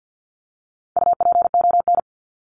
Morse code, heard as a keyed tone, about 50 wpm, 700 Hz. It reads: UPON
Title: Top 1001-1250 Words 50wpm